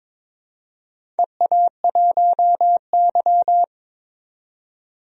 Morse code, heard as a keyed tone, about 22 wpm, 700 Hz. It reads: EA1Y
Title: DXCC-100 Call Signs - 4 Characters - 22wpm